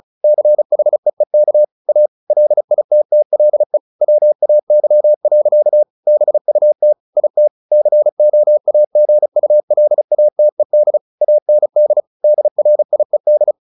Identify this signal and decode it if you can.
Morse code, heard as a keyed tone, about 34 wpm, 600 Hz. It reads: CHEEK A LITTLE WAY. BUT IT COAGULATED AND DRIED
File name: the old man and the sea / the old man and the sea 1 35wpm